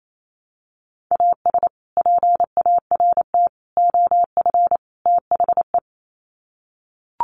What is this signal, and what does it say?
Morse code, heard as a keyed tone, about 28 wpm, 700 Hz. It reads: AS PART OF THE
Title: Sets of 4 Words 28wpm